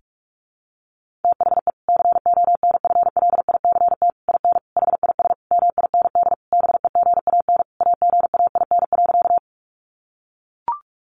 Morse code, heard as a keyed tone, about 44 wpm, 700 Hz. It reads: THE CONFLICT IN HIS MIND BEGAN AGAIN.
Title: Sentences from Top 1300 Words 45wpm